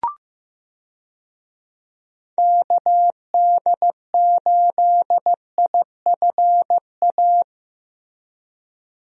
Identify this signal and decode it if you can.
Morse code, heard as a keyed tone, about 16 wpm, 700 Hz. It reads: KD8IFA
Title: USA Call Signs - 6 Characters - Repeated 3x - 15wpm